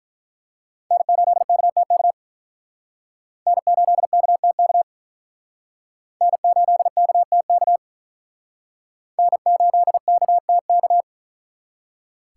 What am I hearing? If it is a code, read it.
Morse code, heard as a keyed tone, about 44 wpm, 700 Hz. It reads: N8KTK N8KTK N8KTK N8KTK
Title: Speed Racing - USA Call Signs 35wpm